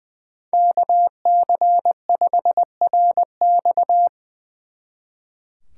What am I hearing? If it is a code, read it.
Morse code, heard as a keyed tone, about 23 wpm, 700 Hz. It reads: KC5RX
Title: Speed Racing - USA 2x2 Call Signs 20wpm